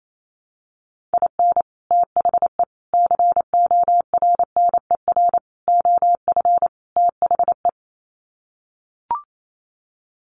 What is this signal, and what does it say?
Morse code, heard as a keyed tone, about 28 wpm, 700 Hz. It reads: IN THE CORNER OF THE
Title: Sets of 5 Words Encore 28wpm